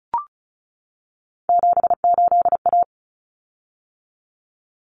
Morse code, heard as a keyed tone, about 34 wpm, 700 Hz. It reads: 78A
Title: Common Cut Numbers - 3-Digit 35wpm